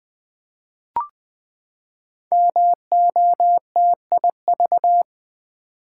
Morse code, heard as a keyed tone, about 20 wpm, 700 Hz. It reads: MOTIV
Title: Build A Word - 5 Letters At A Time - 10-Letter Words - 20wpm